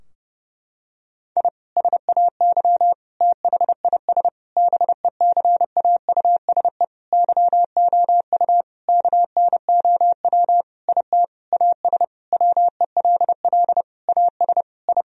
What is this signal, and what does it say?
Morse code, heard as a keyed tone, about 30 wpm, 700 Hz. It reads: I SAY THIS BECAUSE YOU KNOW IT AS WELL AS I
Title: Sentences from Top 1000 Words - Review Part 3 - 30wpm